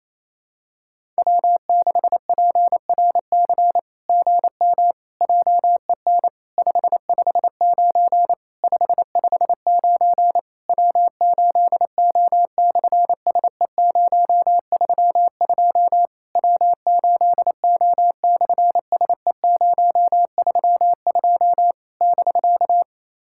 Morse code, heard as a keyed tone, about 27 wpm, 700 Hz. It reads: W6PRC GM JEN 559 559 W8O/SE032 W8O/SE032 <BK>
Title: SOTA QSO Practice 28wpm